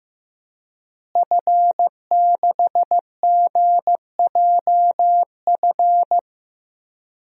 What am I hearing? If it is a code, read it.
Morse code, heard as a keyed tone, about 15 wpm, 700 Hz. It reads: F6GJF
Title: DXCC-100 Call Signs - 5 Characters - Repeated 3x - 15wpm